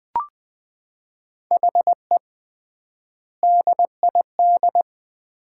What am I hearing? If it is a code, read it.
Morse code, heard as a keyed tone, about 20 wpm, 700 Hz. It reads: HE DID
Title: 3x Word-Spacing - Sets of 4 Words - 20wpm